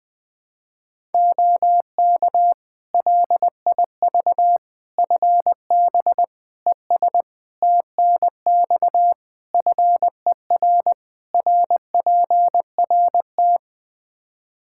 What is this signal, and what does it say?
Morse code, heard as a keyed tone, about 20 wpm, 700 Hz. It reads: OK LIV FB ES TNX FER RPRT
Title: Practice QSO - Exchange 2 - Info - 20wpm